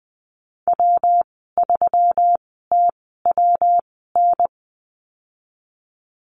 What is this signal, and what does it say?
Morse code, heard as a keyed tone, about 20 wpm, spaced as for 15 wpm, 700 Hz. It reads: W3TWN
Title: 154 Lesson 25g - Letter p - Call Signs 20wpm